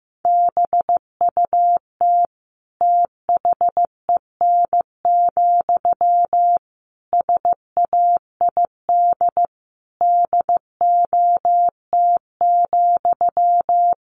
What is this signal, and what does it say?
Morse code, heard as a keyed tone, about 14 wpm, 700 Hz. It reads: BUT THEN, SAID DOT,
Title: Sentences from Top 1500 Words 15wpm